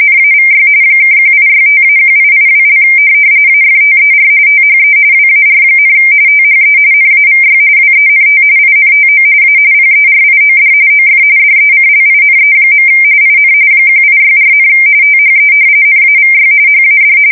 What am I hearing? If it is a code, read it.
RTTY, about 45.45 baud. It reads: NAVY (1865-1882) AFTER THE WAR, THE NAVY WENT INTO A PERIOD OF DECLINE. IN 1864, THE NAVY HAD
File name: fsk170 / usnavy3